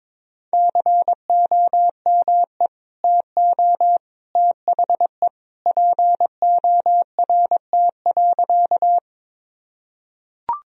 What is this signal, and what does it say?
Morse code, heard as a keyed tone, about 22 wpm, 700 Hz. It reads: COME TO THE PORT.
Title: Sentences from Top 1000 Words - Review Part 1 - 22wpm